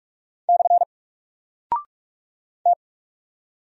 Morse code, heard as a keyed tone, about 44 wpm, 700 Hz. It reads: / T
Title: Single Character (Letter:Number:Punctuation:Prosign) Alt 45wpm